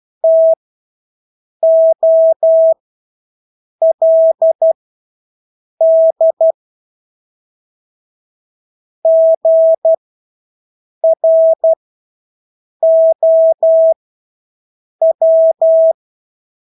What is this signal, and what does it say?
Morse code, heard as a keyed tone, about 12 wpm, spaced as for 6 wpm, 650 Hz. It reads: TOLD GROW
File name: Engelske ord 12-6 wpm